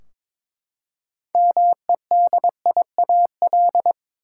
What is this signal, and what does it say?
Morse code, heard as a keyed tone, about 22 wpm, 700 Hz. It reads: MEDIAL
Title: K1USN SST QSO Practice 22wpm